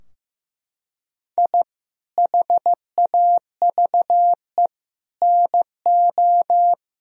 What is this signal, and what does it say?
Morse code, heard as a keyed tone, about 15 wpm, 700 Hz. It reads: I HAVE NO
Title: Sentences from Top 1000 Words - Review Part 4 - 15wpm